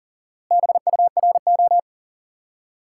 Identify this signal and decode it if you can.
Morse code, heard as a keyed tone, about 37 wpm, 700 Hz. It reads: BURO
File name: IQE Course 3 - Lesson 89 - YAGI - Two Elements - 40wpm